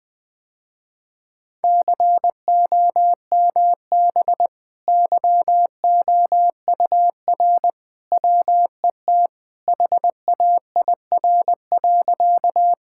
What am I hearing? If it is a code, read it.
Morse code, heard as a keyed tone, about 20 wpm, 700 Hz. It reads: COMB YOUR WET HAIR.
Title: Easy Sentences with Hints - 20wpm